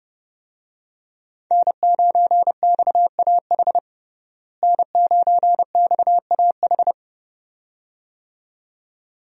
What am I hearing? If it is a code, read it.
Morse code, heard as a keyed tone, about 32 wpm, 700 Hz. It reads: N9XAH N9XAH
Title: USA Call Signs - 5 Characters - Repeated 2x - 30wpm